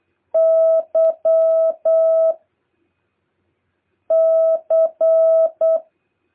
Morse code, heard as a keyed tone, about 8 wpm, 650 Hz. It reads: Y C
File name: jh8cbh / 2023cw